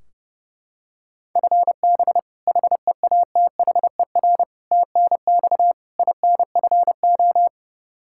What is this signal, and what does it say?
Morse code, heard as a keyed tone, about 30 wpm, 700 Hz. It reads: FB HEATHER TNX INFO
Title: Practice QSO - Exchange 3 - More Info - 30wpm